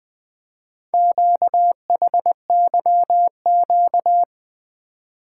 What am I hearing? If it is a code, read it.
Morse code, heard as a keyed tone, about 20 wpm, 700 Hz. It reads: QHYQ
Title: Four Letter-Number Combinations - 20wpm